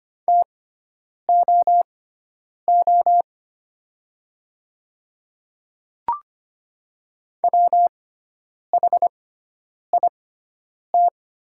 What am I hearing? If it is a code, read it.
Morse code, heard as a keyed tone, about 25 wpm, spaced as for 9 wpm, 700 Hz. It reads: TOO WHIT
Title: 6x Character Spacing - Top 300 Words - 25wpm